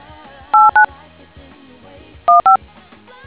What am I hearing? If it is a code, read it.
DTMF, heard as telephone touch tones, tones of about 130 ms. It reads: 89 15